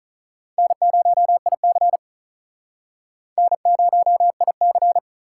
Morse code, heard as a keyed tone, about 43 wpm, 700 Hz. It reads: N0IC N0IC
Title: Speed Racing - USA 1x2 Call Signs 35wpm